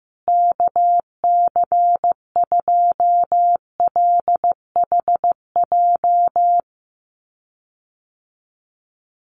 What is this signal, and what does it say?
Morse code, heard as a keyed tone, about 15 wpm, 700 Hz. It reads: KC2LHJ
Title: USA Call Signs Repeated 2x 15wpm